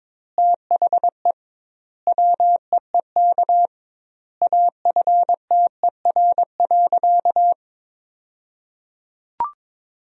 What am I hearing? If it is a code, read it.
Morse code, heard as a keyed tone, about 22 wpm, 700 Hz. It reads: THE WEEK AFTER.
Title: 2x Word-Spacing - Sentences from Top 500 Words - 22wpm